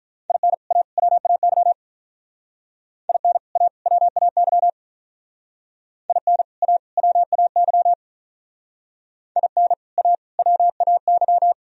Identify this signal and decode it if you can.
Morse code, heard as a keyed tone, about 45 wpm, 690 Hz. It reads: IN A WAY IN A WAY IN A WAY IN A WAY